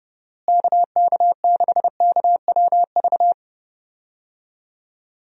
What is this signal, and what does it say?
Morse code, heard as a keyed tone, about 31 wpm, 700 Hz. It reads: KK6KWV
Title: USA Call Signs - 6 Characters - Repeated 3x - 30wpm